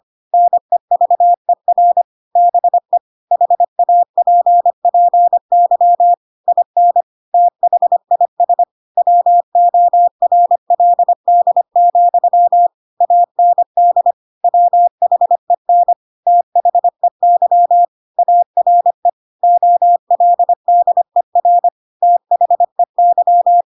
Morse code, heard as a keyed tone, about 25 wpm, 700 Hz. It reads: NEVER BE HAPPY IN THIS WORLD, AND WHEN THEY ARE OLDER THEY
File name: Pinocchio / Pinocchio ch4 25 25 ik1hge